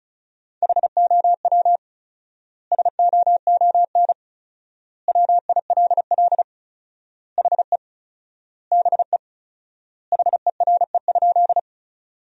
Morse code, heard as a keyed tone, about 36 wpm, 700 Hz. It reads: HOW SOON WILL HE BE HERE?
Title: 4x Word-Spacing - Sentences from Top 400 Words - 35wpm